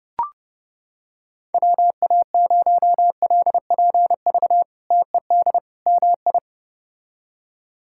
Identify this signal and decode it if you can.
Morse code, heard as a keyed tone, about 29 wpm, 700 Hz. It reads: WA0LPV TED MI
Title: K1USN SST QSO Practice - 30wpm